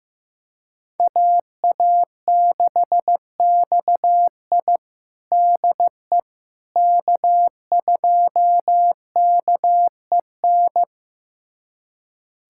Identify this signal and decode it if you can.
Morse code, heard as a keyed tone, about 15 wpm, 700 Hz. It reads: AA6XI DE K2KEN
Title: USA Callsign DE Callsign - 5 Characters - 15wpm